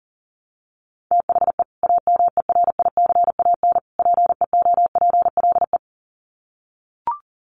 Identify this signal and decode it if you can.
Morse code, heard as a keyed tone, about 40 wpm, 700 Hz. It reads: THE AMERICAN PEOPLE
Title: Sets of 3 Words 40wpm